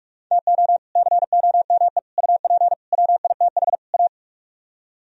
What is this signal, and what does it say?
Morse code, heard as a keyed tone, about 46 wpm, 700 Hz. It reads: TO COME UP WITH A